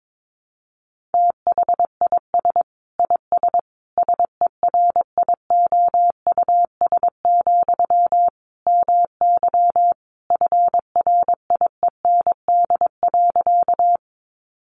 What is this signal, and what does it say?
Morse code, heard as a keyed tone, about 22 wpm, 700 Hz. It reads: THIS IS SERIOUS, MY FRIEND.
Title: Sentences from Top 1100 Words 22wpm